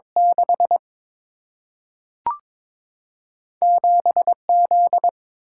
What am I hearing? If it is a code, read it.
Morse code, heard as a keyed tone, about 22 wpm, 700 Hz. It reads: 6 7 Z
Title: Single-Letter Single-Number 22wpm